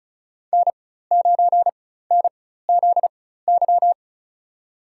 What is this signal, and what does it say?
Morse code, heard as a keyed tone, about 36 wpm, spaced as for 16 wpm, 700 Hz. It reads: N9NZY
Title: USA Call Signs - 5 Characters - 4x Character Spacing - 35wpm